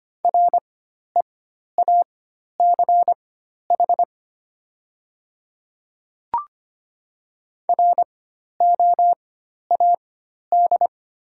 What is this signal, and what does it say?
Morse code, heard as a keyed tone, about 25 wpm, spaced as for 12 wpm, 700 Hz. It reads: REACH ROAD
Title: 4x Character Spacing - Top 500 Words - 25wpm